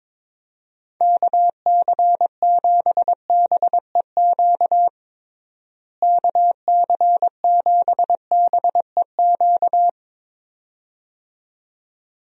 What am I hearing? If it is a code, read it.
Morse code, heard as a keyed tone, about 22 wpm, 700 Hz. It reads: KC7BEQ KC7BEQ
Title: USA Call Signs - 6 Characters - Repeated 2x - 22wpm